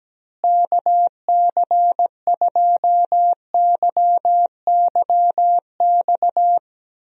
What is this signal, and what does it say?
Morse code, heard as a keyed tone, about 17 wpm, 700 Hz. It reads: KC2YYX
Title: USA Call Signs - 6 Characters - Repeated 2x - 17wpm